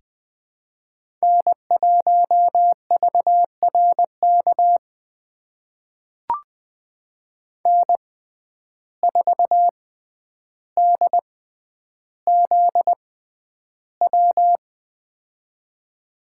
Morse code, heard as a keyed tone, about 20 wpm, spaced as for 10 wpm, 700 Hz. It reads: N1VRK N4DZW
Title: USA Call Signs - 5 Characters - 6x Character Spacing - 20wpm